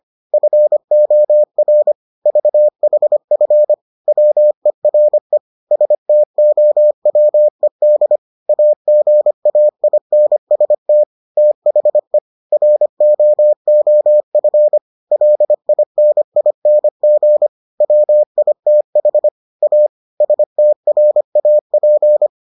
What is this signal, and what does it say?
Morse code, heard as a keyed tone, about 25 wpm, 600 Hz. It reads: FOR VHF WERE STOWED AGAINST THE ROOF LINING WITH A STRAP